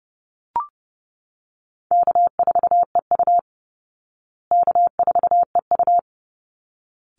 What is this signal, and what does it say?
Morse code, heard as a keyed tone, about 29 wpm, 700 Hz. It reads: K4EU K4EU
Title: Call Signs Repeated 3x - 30wpm